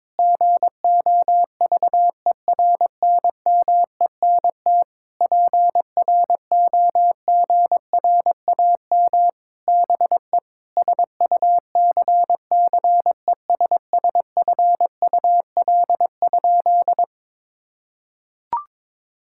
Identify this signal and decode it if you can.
Morse code, heard as a keyed tone, about 22 wpm, 700 Hz. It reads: GOVERNMENT PROGRAM BE SUCCESSFUL?
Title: Sentences from Top 1000 Words - Review Part 5 - 22wpm